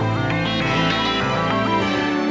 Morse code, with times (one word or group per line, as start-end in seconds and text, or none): none